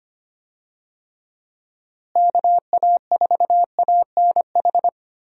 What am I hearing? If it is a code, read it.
Morse code, heard as a keyed tone, about 25 wpm, 700 Hz. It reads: KA4ANH
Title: USA Call Signs - 6 Characters - Repeated 2x - 25wpm